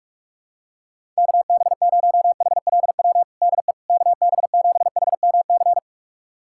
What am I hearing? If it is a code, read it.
Morse code, heard as a keyed tone, about 46 wpm, 700 Hz. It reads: KB0HLW DE KB7HMC
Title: USA Callsign DE Callsign - 6 Characters - 45wpm